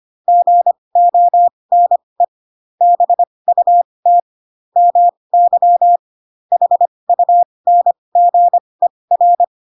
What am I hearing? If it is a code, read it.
Morse code, heard as a keyed tone, about 25 wpm, spaced as for 20 wpm, 700 Hz. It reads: GONE BUT MY HUNGER
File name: Pinocchio / Pinocchio ch7 15 25 ik1hge